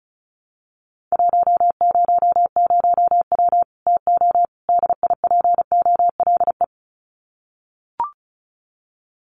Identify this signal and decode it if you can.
Morse code, heard as a keyed tone, about 34 wpm, 700 Hz. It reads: 100W TO DIPOLE
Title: Common QSO Phrases 35wpm